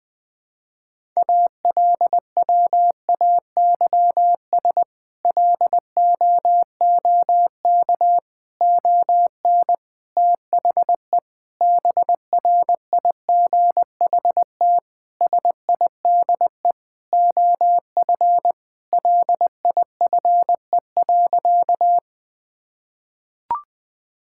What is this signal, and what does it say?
Morse code, heard as a keyed tone, about 20 wpm, 700 Hz. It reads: ALWAYS LOOK ON THE BRIGHT SIDE OF LIFE.
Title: Sentences with 8-Word Phrases - Spoken Beginning - 20wpm